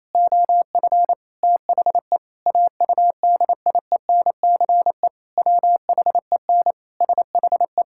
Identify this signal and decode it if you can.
Morse code, heard as a keyed tone, about 28 wpm, 700 Hz. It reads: OF THE AUDIENCE WHEN SHE
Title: Sentences from Top 1000 Words - Review Part 5 - 28wpm